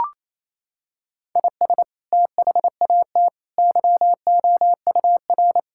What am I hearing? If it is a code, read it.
Morse code, heard as a keyed tone, about 27 wpm, 700 Hz. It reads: IS THAT YOUR
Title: Sentences from Top 1000 Words - Review Part 5 - 28wpm